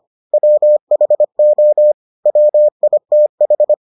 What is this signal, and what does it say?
Morse code, heard as a keyed tone, about 25 wpm, 600 Hz. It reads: WHO WITH